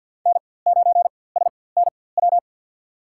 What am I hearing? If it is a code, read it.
Morse code, heard as a keyed tone, about 47 wpm, spaced as for 23 wpm, 700 Hz. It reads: N9SNW
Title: USA Call Signs - 5 Characters - 4x Character Spacing - 50wpm